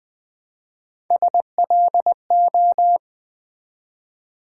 Morse code, heard as a keyed tone, about 20 wpm, 700 Hz. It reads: SLO
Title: Build A Word - 1 Letter At A Time - 4-Letter Words - 20wpm